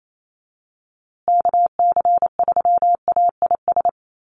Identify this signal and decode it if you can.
Morse code, heard as a keyed tone, about 28 wpm, 700 Hz. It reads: KC3AIS
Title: USA Call Signs Repeated 3x 28wpm